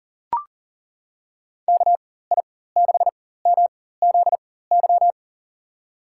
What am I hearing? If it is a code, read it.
Morse code, heard as a keyed tone, about 40 wpm, spaced as for 19 wpm, 700 Hz. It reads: KI6MZY
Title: USA Call Signs - 6 Characters - 4x Character Spacing - 40wpm